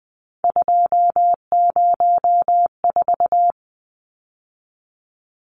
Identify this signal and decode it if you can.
Morse code, heard as a keyed tone, about 20 wpm, 700 Hz. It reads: 204
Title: Numbers - 3-Digit 20wpm